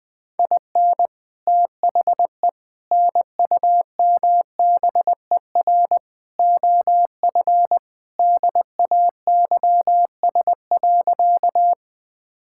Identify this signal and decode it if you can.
Morse code, heard as a keyed tone, about 20 wpm, 700 Hz. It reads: IN THE NUMBER OF DAYS.
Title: Sentences with 4-Word Phrases - 20wpm